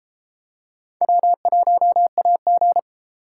Morse code, heard as a keyed tone, about 33 wpm, 700 Hz. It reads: W1AG